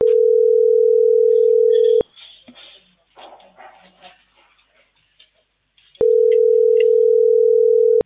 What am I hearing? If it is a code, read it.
Ringback tone (North America), 440 + 480 Hz.